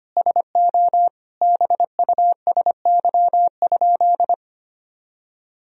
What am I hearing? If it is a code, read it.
Morse code, heard as a keyed tone, about 25 wpm, 700 Hz. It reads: SO BUSY?
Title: Sentences from Top 1000 Words - Review Part 3 - 25wpm